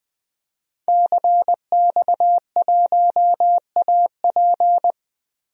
Morse code, heard as a keyed tone, about 20 wpm, 700 Hz. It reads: CX1AP
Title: DXCC-100 Call Signs - 5 Characters - Repeated 2x - 20wpm